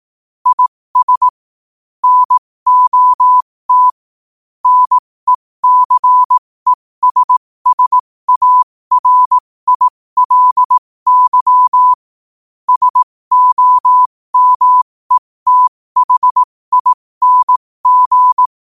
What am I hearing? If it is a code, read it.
Morse code, heard as a keyed tone, about 18 wpm, spaced as for 15 wpm, 1000 Hz. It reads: IS NOT NECESSARILY SOMETHING
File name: qotdfiles / 15wpm